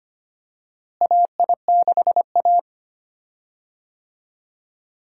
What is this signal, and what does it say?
Morse code, heard as a keyed tone, about 25 wpm, 700 Hz. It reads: AI6A
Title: USA Call Signs - 4 Characters - Repeated 2x - 25wpm